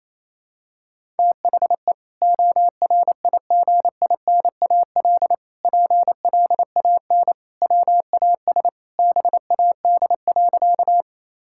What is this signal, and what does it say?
Morse code, heard as a keyed tone, about 28 wpm, 700 Hz. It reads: THE ORIGINAL PLAN WAS BAD.
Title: Sentences from Top 1000 Words - Review Part 6 - 28wpm